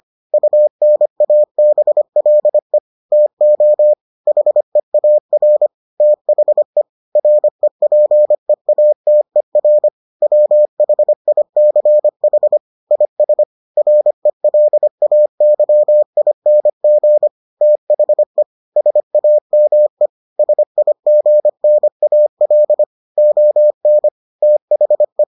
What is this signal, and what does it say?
Morse code, heard as a keyed tone, about 25 wpm, 600 Hz. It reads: UNABLE TO HEAR THE REPEATER WHICH IS RELAYING THE SAME SIGNAL ON THE